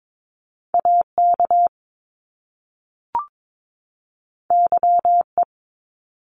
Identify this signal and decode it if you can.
Morse code, heard as a keyed tone, about 22 wpm, 700 Hz. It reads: AK YE
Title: Top Two Letter Combinations in Words 22wpm